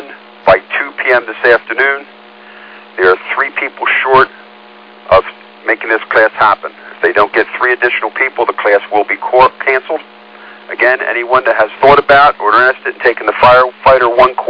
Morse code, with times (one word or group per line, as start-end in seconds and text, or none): none